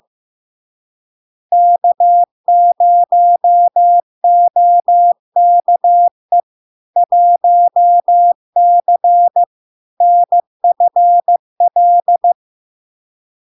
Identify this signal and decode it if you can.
Morse code, heard as a keyed tone, about 15 wpm, 700 Hz. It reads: K0OKE 1C NFL
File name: LARK-FD-2022-Log-in-CW-15wpm-700-Hz